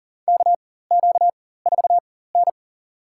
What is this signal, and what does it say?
Morse code, heard as a keyed tone, about 40 wpm, spaced as for 19 wpm, 700 Hz. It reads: KQ4N
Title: USA Call Signs - 4 Characters - 4x Character Spacing - 40wpm